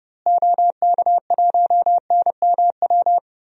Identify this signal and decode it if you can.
Morse code, heard as a keyed tone, about 30 wpm, 700 Hz. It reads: OK1NMW
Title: DXCC-100 Call Signs - 6 Characters - Repeated 3x - 30wpm